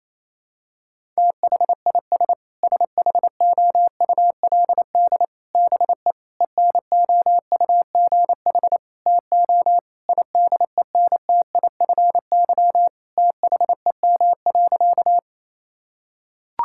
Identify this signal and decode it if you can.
Morse code, heard as a keyed tone, about 28 wpm, 700 Hz. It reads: THIS SHOULD BE ENOUGH TO IDENTIFY THEM.
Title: Sentences from Top 1000 Words - Review Part 6 - 28wpm